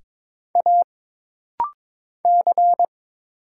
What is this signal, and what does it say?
Morse code, heard as a keyed tone, about 22 wpm, 700 Hz. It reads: A C
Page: This rapid-fire timing is geared to push your ICR (Instant Character Recognition) skills.